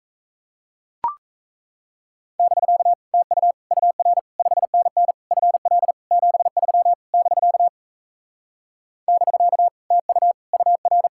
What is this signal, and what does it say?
Morse code, heard as a keyed tone, about 38 wpm, 700 Hz. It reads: <BK> TU UR 5NN FL 73 <BK> <BK> TU UR